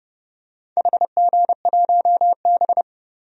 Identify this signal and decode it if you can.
Morse code, heard as a keyed tone, about 30 wpm, 700 Hz. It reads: HG1B
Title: DXCC-100 Call Signs - 4 Characters - Repeated 3x - 30wpm